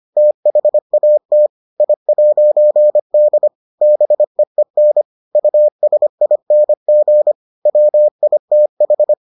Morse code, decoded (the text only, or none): THAT I'D BEEN USING WITH